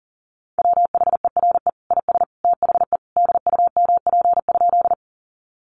Morse code, heard as a keyed tone, about 40 wpm, 700 Hz. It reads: WHERE IS THE DUMP?
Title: Sentences from Top 1300 Words 40wpm